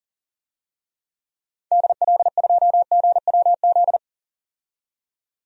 Morse code, heard as a keyed tone, about 40 wpm, 700 Hz. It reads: DL2GWZ
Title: DXCC-100 Call Signs - 6 Characters - 40wpm